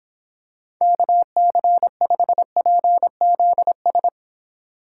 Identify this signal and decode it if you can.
Morse code, heard as a keyed tone, about 26 wpm, 700 Hz. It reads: KC5PZS